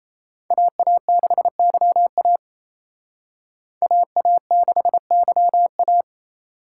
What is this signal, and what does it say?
Morse code, heard as a keyed tone, about 36 wpm, 700 Hz. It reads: AA6YA AA6YA